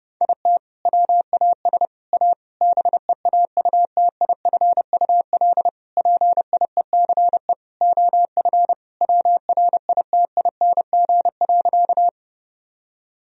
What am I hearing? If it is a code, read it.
Morse code, heard as a keyed tone, about 32 wpm, 700 Hz. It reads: IT WAS A BEAUTIFUL PIECE OF WRITING.
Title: Sentences from Top 1000 Words - Review Part 2 - 30wpm